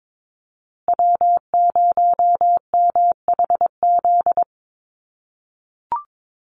Morse code, heard as a keyed tone, about 22 wpm, 700 Hz. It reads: W0MHZ
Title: USA Call Signs Repeated 3x 22wpm